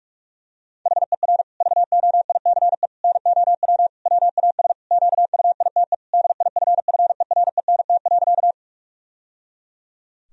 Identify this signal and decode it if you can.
Morse code, heard as a keyed tone, about 43 wpm, 700 Hz. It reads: HER VOICE NOW WAS QUITE DIFFERENT.
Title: Sentences from Top 1000 Words - Review Part 3 - 45wpm